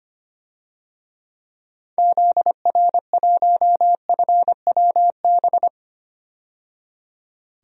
Morse code, heard as a keyed tone, about 25 wpm, 700 Hz. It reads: ZR1FWB
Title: DXCC-100 Call Signs - 6 Characters - 25wpm